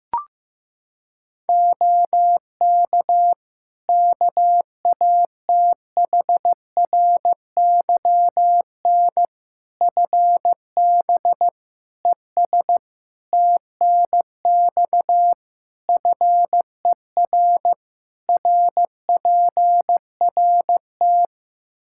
Morse code, heard as a keyed tone, about 15 wpm, 700 Hz. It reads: OK KATHRYN FB ES TNX FER RPRT
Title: Practice QSO - Exchange 2 - Info - 15wpm